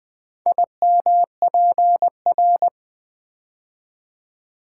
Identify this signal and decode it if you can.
Morse code, heard as a keyed tone, about 20 wpm, 700 Hz. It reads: IMPR
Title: Build A Word - 4 Letters At A Time - 11-Letter Words - 20wpm